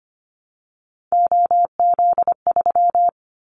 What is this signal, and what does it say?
Morse code, heard as a keyed tone, about 25 wpm, 700 Hz. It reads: OZ3
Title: Callsign Prefixes for Top 100 DXCC - 25wpm